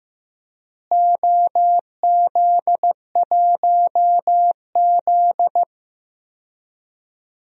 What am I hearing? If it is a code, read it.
Morse code, heard as a keyed tone, about 15 wpm, 700 Hz. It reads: OZ1Z
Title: DXCC-100 Call Signs - 4 Characters - Repeated 2x - 15wpm